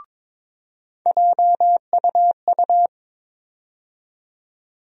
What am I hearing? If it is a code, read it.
Morse code, heard as a keyed tone, about 22 wpm, 700 Hz. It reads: JUU
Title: USA Call Sign Postfixes - 3 Letters 22wpm